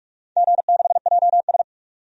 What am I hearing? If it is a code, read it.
Morse code, heard as a keyed tone, about 46 wpm, 700 Hz. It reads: G6JS
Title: DXCC-100 Call Signs - 4 Characters - 45wpm